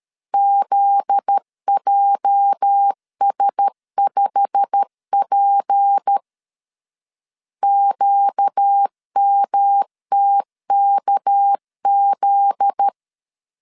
Morse code, heard as a keyed tone, about 13 wpm, 800 Hz. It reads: ZJS5P QMTKZ
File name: CW / DET / 1 9 1500C S B 12wpm 3819